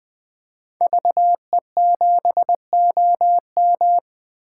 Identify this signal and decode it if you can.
Morse code, heard as a keyed tone, about 21 wpm, 700 Hz. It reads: VE7OM
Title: 2023 SOTA Chaser Call Signs - 20wpm